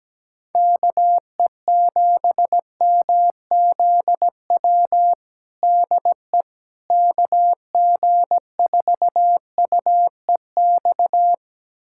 Morse code, heard as a keyed tone, about 17 wpm, 700 Hz. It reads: KE7MZW DE KG4UEX
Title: USA Callsign DE Callsign - 6 Characters - 17wpm